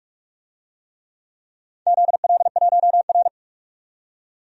Morse code, heard as a keyed tone, about 45 wpm, 700 Hz. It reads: ZL1R
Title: DXCC-100 Call Signs - 4 Characters - 45wpm